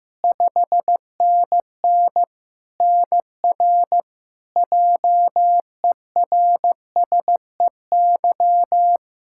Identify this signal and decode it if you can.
Morse code, heard as a keyed tone, about 15 wpm, 700 Hz. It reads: 5NN NR JERSEY
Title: Practice QSO - Exchange 1 - Beginning - 15wpm